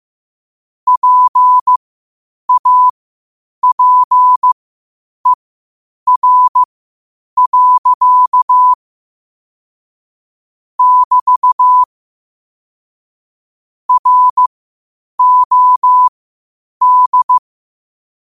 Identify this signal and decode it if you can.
Morse code, heard as a keyed tone, about 15 wpm, spaced as for 8 wpm, 1000 Hz. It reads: PAPER. <BT> ROD
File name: 8wpm